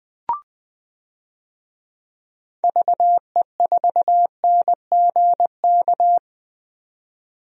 Morse code, heard as a keyed tone, about 20 wpm, 700 Hz. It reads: VE4NGK
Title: DXCC-100 Call Signs - 6 Characters - 20wpm